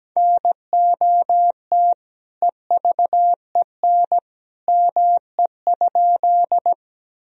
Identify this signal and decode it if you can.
Morse code, heard as a keyed tone, about 17 wpm, 700 Hz. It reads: NOT EVEN ME?
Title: Sentences from Top 1000 Words - Review Part 1 - 17wpm